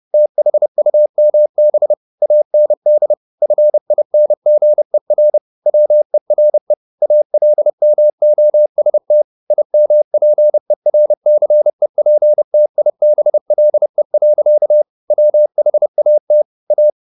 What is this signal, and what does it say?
Morse code, heard as a keyed tone, about 30 wpm, 600 Hz. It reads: THUMB AND FINGER WERE ALMOST IMPERCEPTIBLE. WHAT A
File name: the old man and the sea / the old man and the sea 1 30wpm